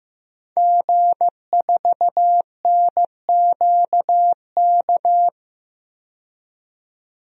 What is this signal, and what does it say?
Morse code, heard as a keyed tone, about 15 wpm, 700 Hz. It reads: G4NQK
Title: DXCC-100 Call Signs - 5 Characters - 15wpm